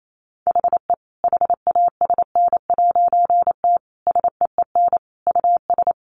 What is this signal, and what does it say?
Morse code, heard as a keyed tone, about 28 wpm, 700 Hz. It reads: HE HASN'T SEEN US
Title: A Sound Of Thunder 28wpm